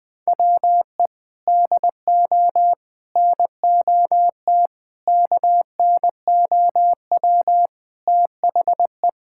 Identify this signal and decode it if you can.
Morse code, heard as a keyed tone, about 20 wpm, 700 Hz. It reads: WE DO NOT KNOW THE
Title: Sentences from Top 1000 Words - Review Part 6 - 20wpm